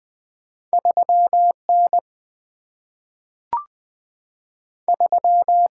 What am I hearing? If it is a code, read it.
Morse code, heard as a keyed tone, about 20 wpm, 690 Hz. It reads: 3 N 3
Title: Two Letter-Number Combinations - 20wpm